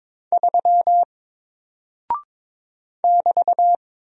Morse code, heard as a keyed tone, about 22 wpm, 700 Hz. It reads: 3 <BT>
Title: Single Character (Letter:Number:Punctuation:Prosign) Alt 22wpm